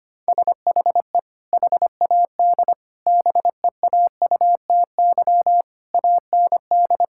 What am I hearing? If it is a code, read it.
Morse code, heard as a keyed tone, about 25 wpm, 700 Hz. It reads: SHE HAD BEAUTY AND
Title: Sentences from Top 1000 Words - Review Part 5 - 25wpm